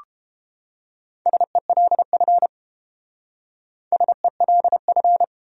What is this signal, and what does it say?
Morse code, heard as a keyed tone, about 29 wpm, 700 Hz. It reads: SELF SELF